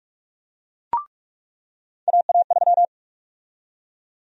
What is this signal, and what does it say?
Morse code, heard as a keyed tone, about 46 wpm, 700 Hz. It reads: AA3
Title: USA Call Sign Prefixes - 45wpm